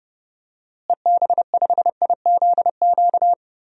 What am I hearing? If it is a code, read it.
Morse code, heard as a keyed tone, about 30 wpm, 700 Hz. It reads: EB5IZQ
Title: DXCC-100 Call Signs - 6 Characters - Repeated 3x - 30wpm